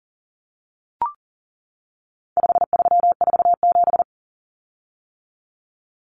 Morse code, heard as a keyed tone, about 40 wpm, 700 Hz. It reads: 5347
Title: Common Cut Numbers - 4-Digit 40wpm